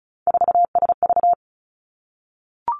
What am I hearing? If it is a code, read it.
Morse code, heard as a keyed tone, about 36 wpm, 700 Hz. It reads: 4SV
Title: Three Challenging Letter-Number Combinations 35wpm